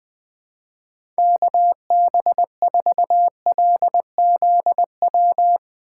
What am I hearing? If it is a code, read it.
Morse code, heard as a keyed tone, about 21 wpm, 700 Hz. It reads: KB4LZW
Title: USA Call Signs - 6 Characters - 2x Character Spacing - 20wpm